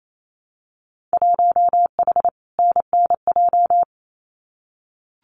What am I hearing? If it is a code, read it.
Morse code, heard as a keyed tone, about 28 wpm, 700 Hz. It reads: 1H NNJ
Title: Winter Field Day Exchanges 28wpm